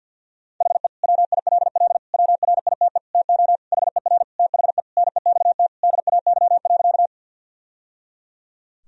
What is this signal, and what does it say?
Morse code, heard as a keyed tone, about 46 wpm, 700 Hz. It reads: HE WILL WRITE TO HER THE NEXT DAY.
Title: Sentences from Top 1000 Words - Review Part 1 - 50wpm